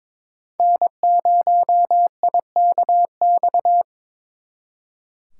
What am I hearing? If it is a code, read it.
Morse code, heard as a keyed tone, about 26 wpm, 700 Hz. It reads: N0IKX
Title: Speed Racing - USA 1x3 Call Signs 22wpm